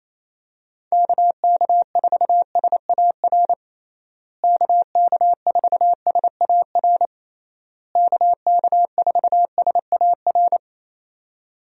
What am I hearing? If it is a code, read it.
Morse code, heard as a keyed tone, about 28 wpm, 700 Hz. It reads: KK4SAR KK4SAR KK4SAR
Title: USA Call Signs - 6 Characters - Repeated 3x - 28wpm